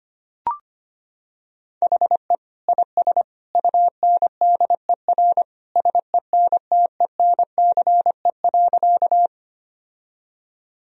Morse code, heard as a keyed tone, about 25 wpm, 700 Hz. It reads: HE IS UNDER SENTENCE.
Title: Sentences from Top 1000 Words - Review Part 1 - 25wpm